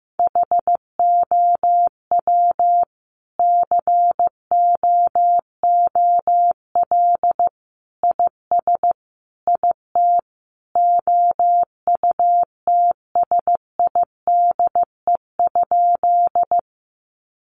Morse code, heard as a keyed tone, about 15 wpm, 700 Hz. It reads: HOW COOL IS IT OUTSIDE?
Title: Sentences from Top 1200 Words 15wpm